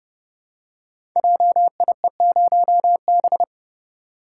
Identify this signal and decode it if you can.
Morse code, heard as a keyed tone, about 30 wpm, 700 Hz. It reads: JIE0B
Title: Code Groups - 5 Letter:Numbers - 30wpm